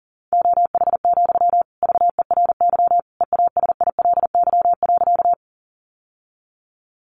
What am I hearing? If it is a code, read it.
Morse code, heard as a keyed tone, about 40 wpm, 700 Hz. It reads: OH, VERY EASILY.
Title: Sentences from Top 900 Words 40wpm